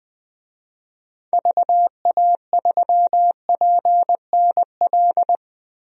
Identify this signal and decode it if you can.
Morse code, heard as a keyed tone, about 20 wpm, 700 Hz. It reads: VA3PNL
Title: DXCC-100 Call Signs - 6 Characters - 20wpm